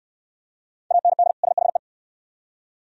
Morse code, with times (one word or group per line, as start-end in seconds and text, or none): 0.9-1.3 AND
1.4-1.8 SHE